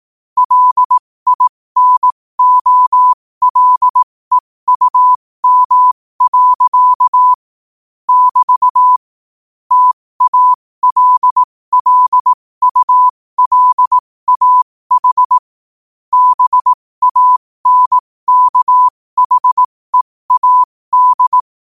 Morse code, heard as a keyed tone, about 18 wpm, spaced as for 15 wpm, 1000 Hz. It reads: LINOLEUM. <BT> TALLULAH BANKHEAD
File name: qotdfiles / 15wpm